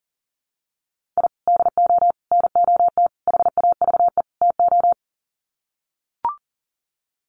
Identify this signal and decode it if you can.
Morse code, heard as a keyed tone, about 40 wpm, 700 Hz. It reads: I DO NOT HAVE TO
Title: Sets of 5 Words 40wpm